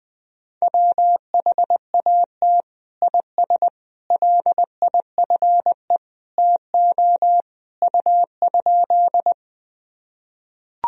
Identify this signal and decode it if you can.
Morse code, heard as a keyed tone, about 21 wpm, 700 Hz. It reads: WHAT IS LIFE TO U?
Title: Sentences from Top 300 Words and Abbreviations 20wpm